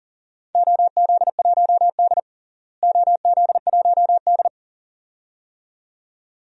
Morse code, heard as a keyed tone, about 40 wpm, 700 Hz. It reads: OZ1D OZ1D
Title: DXCC-100 Call Signs - 4 Characters - Repeated 2x - 40wpm